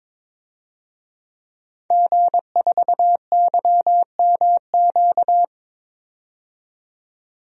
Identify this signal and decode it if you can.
Morse code, heard as a keyed tone, about 22 wpm, 700 Hz. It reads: G4YMQ
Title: DXCC-100 Call Signs - 5 Characters - 22wpm